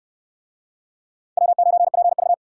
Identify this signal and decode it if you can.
Morse code, heard as a keyed tone, about 67 wpm, 700 Hz. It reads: W9PV